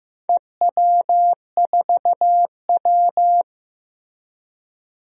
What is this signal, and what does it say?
Morse code, heard as a keyed tone, about 15 wpm, 700 Hz. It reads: EW4W
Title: DXCC-100 Call Signs - 4 Characters - Repeated 3x - 15wpm